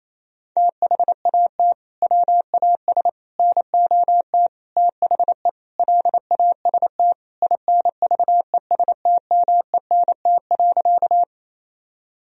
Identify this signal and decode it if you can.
Morse code, heard as a keyed tone, about 28 wpm, 700 Hz. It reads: THAT WAS NOT THE LAST INVESTMENT.
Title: Sentences from Top 1000 Words - Review Part 4 - 28wpm